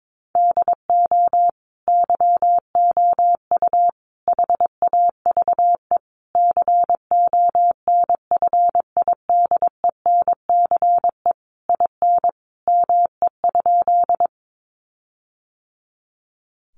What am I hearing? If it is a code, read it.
Morse code, heard as a keyed tone, about 22 wpm, 700 Hz. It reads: DO YOU HAVE CONFIDENCE IN ME?
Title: Sentences from Top 1500 Words 22wpm